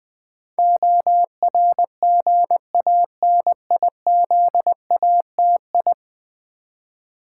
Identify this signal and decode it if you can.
Morse code, heard as a keyed tone, about 20 wpm, 700 Hz. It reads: ORGANIZATI
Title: Build A Word - 5 Letters At A Time - 12-Letter Words - 20wpm